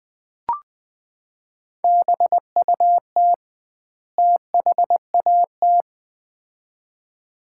Morse code, heard as a keyed tone, about 21 wpm, 700 Hz. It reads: BUT THAT
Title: 2x Word-Spacing - Sets of 2 Words - 20wpm